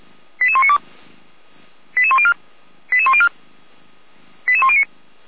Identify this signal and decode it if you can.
Five-tone selective call (ZVEI), heard as quick runs of five tones, tones of about 70 ms. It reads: 88192 88104 88104 88109